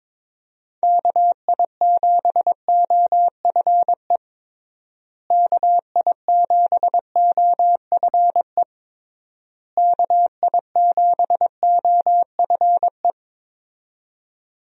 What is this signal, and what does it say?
Morse code, heard as a keyed tone, about 22 wpm, 700 Hz. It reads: KI7OFE KI7OFE KI7OFE
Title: USA Call Signs - 6 Characters - Repeated 3x - 22wpm